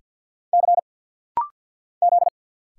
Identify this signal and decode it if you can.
Morse code, heard as a keyed tone, about 48 wpm, 700 Hz. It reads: C Z
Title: Single Letter-Number - Rapid-Fire 50wpm